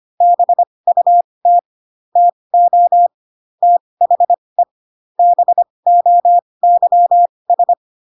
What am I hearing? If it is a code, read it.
Morse code, heard as a keyed tone, about 25 wpm, spaced as for 20 wpm, 700 Hz. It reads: BUT TO THE BOYS